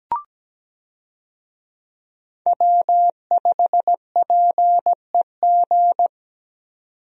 Morse code, heard as a keyed tone, about 17 wpm, 700 Hz. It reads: W5PEG
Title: USA Call Signs - 5 Characters - 17wpm